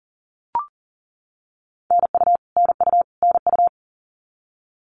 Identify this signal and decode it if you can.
Morse code, heard as a keyed tone, about 40 wpm, 700 Hz. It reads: NU NU NU
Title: Canadian Provinces and Territories - Repeated 3x - 40wpm